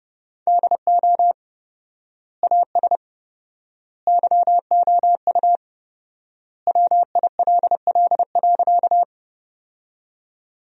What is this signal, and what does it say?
Morse code, heard as a keyed tone, about 30 wpm, 700 Hz. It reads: DO AS YOU WILL.
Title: 4x Word-Spacing - Sentences from Top 100 Words - 30wpm